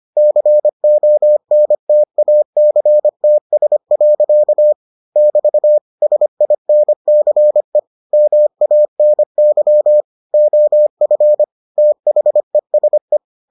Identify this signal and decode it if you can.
Morse code, heard as a keyed tone, about 25 wpm, 600 Hz. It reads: CONTACTS. <BT> SINCE MANY OF THESE